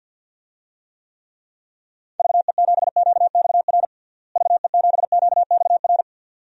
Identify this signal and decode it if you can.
Morse code, heard as a keyed tone, about 51 wpm, 700 Hz. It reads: VE7QXR VE7QXR
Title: DXCC-100 Call Signs - 6 Characters - Repeated 2x - 50wpm